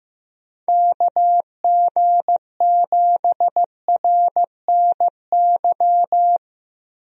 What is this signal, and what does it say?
Morse code, heard as a keyed tone, about 15 wpm, 700 Hz. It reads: KG7RNY
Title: USA Call Signs - 6 Characters - Repeated 3x - 15wpm